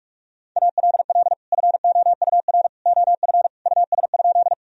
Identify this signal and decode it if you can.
Morse code, heard as a keyed tone, about 42 wpm, 700 Hz. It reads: ALL FOUR OF US?
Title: Sentences from Top 1000 Words - Review Part 1 - 45wpm